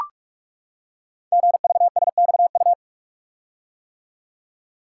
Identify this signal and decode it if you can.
Morse code, heard as a keyed tone, about 44 wpm, 700 Hz. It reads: GVSXU
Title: Code Groups - 5 Letters - 45wpm